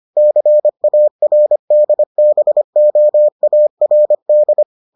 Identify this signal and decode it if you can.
Morse code, heard as a keyed tone, about 25 wpm, 600 Hz. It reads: CARDBOARD